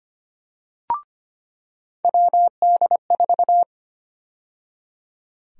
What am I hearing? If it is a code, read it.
Morse code, heard as a keyed tone, about 25 wpm, 700 Hz. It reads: WD4
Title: USA Call Sign Prefixes - 25wpm